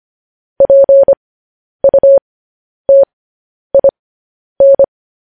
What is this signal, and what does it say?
Morse code, heard as a keyed tone, about 25 wpm, spaced as for 10 wpm, 550 Hz. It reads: PUTIN
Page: news headlines in morse code at 10 wpm